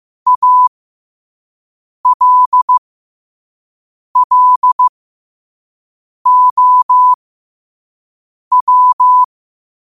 Morse code, heard as a keyed tone, about 15 wpm, spaced as for 5 wpm, 1000 Hz. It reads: ALLOW
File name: qotdfiles / 5wpm